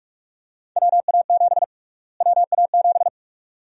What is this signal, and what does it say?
Morse code, heard as a keyed tone, about 46 wpm, 700 Hz. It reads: WA7 WA7
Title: USA Call Sign Prefixes - Repeated 2x - 45wpm